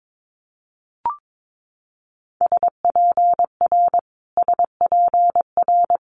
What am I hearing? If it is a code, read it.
Morse code, heard as a keyed tone, about 22 wpm, 700 Hz. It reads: SPR SPR
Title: Multi-letter Phonemes Repeated 3x 22wpm